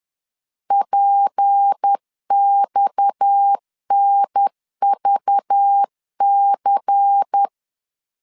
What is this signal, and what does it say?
Morse code, heard as a keyed tone, about 10 wpm, 800 Hz. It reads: PXNVC
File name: LEZIONI / 8 YJ45 / 8 YJ45 10wpm 2549